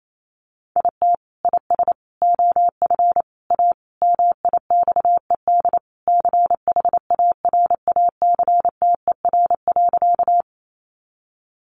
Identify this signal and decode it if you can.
Morse code, heard as a keyed tone, about 28 wpm, 700 Hz. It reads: IT IS OF A MIXED CHARACTER.
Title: Sentences from Top 1500 Words 28wpm